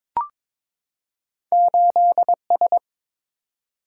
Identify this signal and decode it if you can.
Morse code, heard as a keyed tone, about 22 wpm, 700 Hz. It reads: 8S
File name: practice / 22wpm / Two Letter-Number Combinations - 22wpm